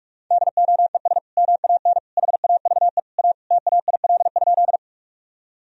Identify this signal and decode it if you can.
Morse code, heard as a keyed tone, about 41 wpm, 700 Hz. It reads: DOES MAN HAVE A TAIL?
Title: Sentences from Top 1000 Words - Review Part 3 - 45wpm